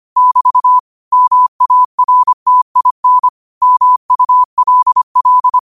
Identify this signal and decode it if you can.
Morse code, heard as a keyed tone, about 25 wpm, 1000 Hz. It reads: <BT> MARTIN MULL